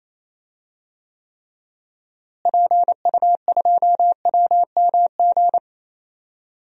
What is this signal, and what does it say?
Morse code, heard as a keyed tone, about 28 wpm, 700 Hz. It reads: PU2WMG
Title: DXCC-100 Call Signs - 6 Characters - Repeated 2x - 28wpm